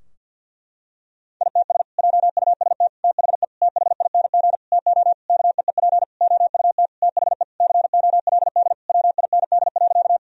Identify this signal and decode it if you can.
Morse code, heard as a keyed tone, about 50 wpm, 700 Hz. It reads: ITS JUST THE THING TO KEEP OUT THE COLD WIND.
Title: Sentences from Top 1000 Words - Review Part 2 - 50wpm